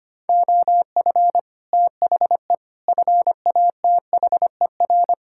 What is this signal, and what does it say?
Morse code, heard as a keyed tone, about 25 wpm, 700 Hz. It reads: OF THE FATHER
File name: practice / 25wpm / Sentences with 9-Word Phrases - 25wpm